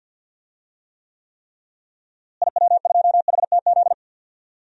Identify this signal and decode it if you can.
Morse code, heard as a keyed tone, about 51 wpm, 700 Hz. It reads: IW2HTZ
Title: DXCC-100 Call Signs - 6 Characters - Repeated 3x - 50wpm